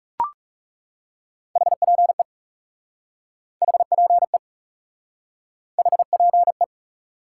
Morse code, heard as a keyed tone, about 37 wpm, 700 Hz. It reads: HPE HPE HPE
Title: Speed-Racing Top 90 QSO Elements 30wpm